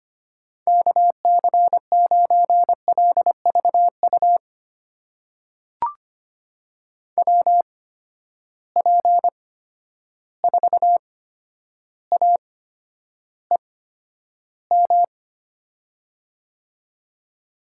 Morse code, heard as a keyed tone, about 25 wpm, spaced as for 11 wpm, 700 Hz. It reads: KC9LVU WP4AEM
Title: USA Call Signs - 6 Characters - 8x Character Spacing - 25wpm